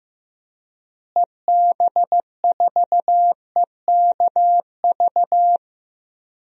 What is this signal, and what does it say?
Morse code, heard as a keyed tone, about 15 wpm, 700 Hz. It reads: EB4EKV
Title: DXCC-100 Call Signs - 6 Characters - Repeated 3x - 15wpm